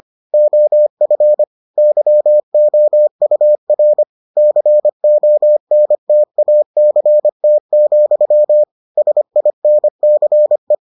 Morse code, heard as a keyed tone, about 25 wpm, 600 Hz. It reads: OF YOUR CONTACT, SINCE